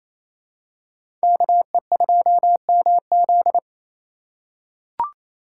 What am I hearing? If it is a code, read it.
Morse code, heard as a keyed tone, about 28 wpm, 700 Hz. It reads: KE2MZ
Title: USA Call Signs - 3x Character Spacing - 28wpm